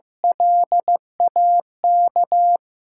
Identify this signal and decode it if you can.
Morse code, heard as a keyed tone, about 15 wpm, 700 Hz. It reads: LAK
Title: USA Call Signs - 5 Characters - 8x Character Spacing - 15wpm